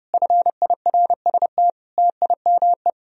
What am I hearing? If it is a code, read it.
Morse code, heard as a keyed tone, about 30 wpm, 700 Hz. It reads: FIRST TIME